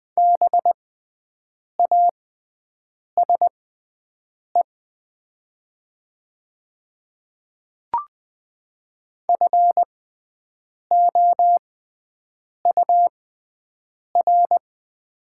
Morse code, heard as a keyed tone, about 20 wpm, spaced as for 7 wpm, 700 Hz. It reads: BASE FOUR
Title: 6x Character Spacing - Top 500 Words - 20wpm